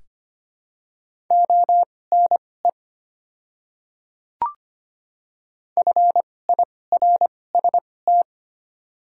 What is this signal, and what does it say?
Morse code, heard as a keyed tone, about 25 wpm, spaced as for 18 wpm, 700 Hz. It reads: ONE FIRST
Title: 2x Character Spacing - Top 100 Words - 25wpm